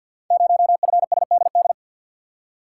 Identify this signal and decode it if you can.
Morse code, heard as a keyed tone, about 52 wpm, 690 Hz. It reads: 0FSDD
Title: Code Groups - 5 Letter:Numbers - 50wpm